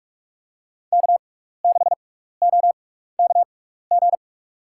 Morse code, heard as a keyed tone, about 45 wpm, spaced as for 15 wpm, 700 Hz. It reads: K6OKG
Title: USA Call Signs - 6x Character Spacing - 45wpm